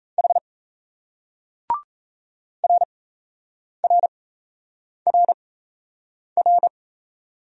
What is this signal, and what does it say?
Morse code, heard as a keyed tone, about 37 wpm, spaced as for 8 wpm, 700 Hz. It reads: H RRRR